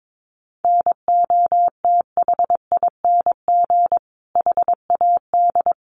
Morse code, heard as a keyed tone, about 22 wpm, 700 Hz. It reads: NOTHING HAD
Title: Sentences from Top 1300 Words 22wpm